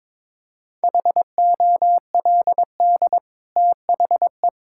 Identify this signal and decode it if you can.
Morse code, heard as a keyed tone, about 21 wpm, 690 Hz. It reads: HOLD THE
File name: practice / 22wpm / Easy Sentences with Hints - 22wpm